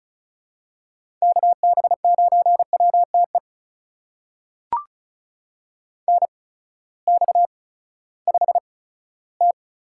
Morse code, heard as a keyed tone, about 34 wpm, spaced as for 17 wpm, 700 Hz. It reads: KB9WTE NX5T
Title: USA Call Signs - 8x Character Spacing - 35wpm